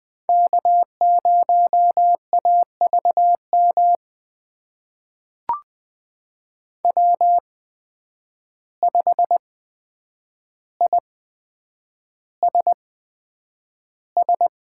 Morse code, heard as a keyed tone, about 20 wpm, spaced as for 9 wpm, 700 Hz. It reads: K0AVM W5ISS
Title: USA Call Signs - 5 Characters - 8x Character Spacing - 20wpm